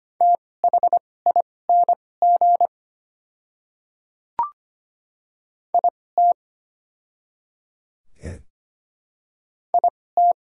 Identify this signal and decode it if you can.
Morse code, heard as a keyed tone, about 25 wpm, spaced as for 18 wpm, 690 Hz. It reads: THING IT IT
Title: 2x Character Spacing - Top 100 Words - 25wpm